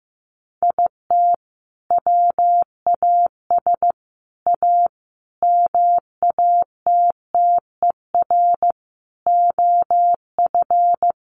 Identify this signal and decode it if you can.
Morse code, heard as a keyed tone, about 15 wpm, 700 Hz. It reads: IT WAS A MATTER OF
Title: Sentences from Top 1500 Words 15wpm